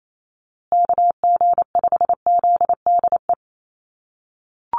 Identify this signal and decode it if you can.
Morse code, heard as a keyed tone, about 28 wpm, 700 Hz. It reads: KG5ZDE
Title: USA Call Signs Repeated 3x 28wpm